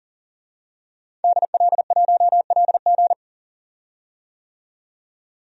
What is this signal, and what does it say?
Morse code, heard as a keyed tone, about 40 wpm, 700 Hz. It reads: DL1LG
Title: DXCC-100 Call Signs - 5 Characters - 40wpm